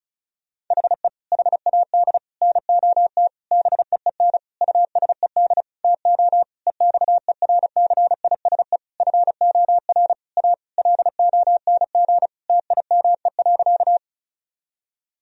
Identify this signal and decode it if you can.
Morse code, heard as a keyed tone, about 36 wpm, 700 Hz. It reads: HE HAD NOT BEEN USED TO EXERCISE FOR A LONG TIME.
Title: Sentences from Top 1000 Words - Review Part 4 - 35wpm